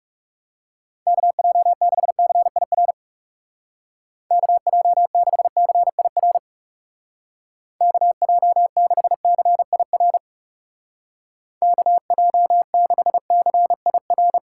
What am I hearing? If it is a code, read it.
Morse code, heard as a keyed tone, about 39 wpm, 700 Hz. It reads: KJ6CIR KJ6CIR KJ6CIR KJ6CIR